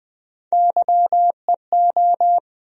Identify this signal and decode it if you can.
Morse code, heard as a keyed tone, about 21 wpm, 700 Hz. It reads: YEO
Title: USA Call Sign Postfixes - 3 Letters 20wpm